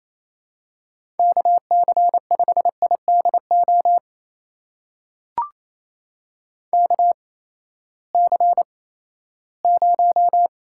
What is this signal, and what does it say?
Morse code, heard as a keyed tone, about 28 wpm, spaced as for 16 wpm, 700 Hz. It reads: KC5IDO KC0
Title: USA Call Signs - 6 Characters - 8x Character Spacing - 28wpm